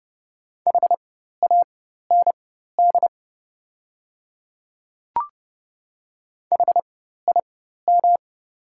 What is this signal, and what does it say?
Morse code, heard as a keyed tone, about 30 wpm, spaced as for 14 wpm, 700 Hz. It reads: HAND HIM
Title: 4x Character Spacing - Top 200 Words - 30wpm